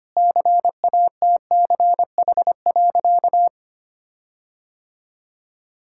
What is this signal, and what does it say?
Morse code, heard as a keyed tone, about 25 wpm, 700 Hz. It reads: CATCH.
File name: practice / 25wpm / All-Age Appropriate Jokes - 25wpm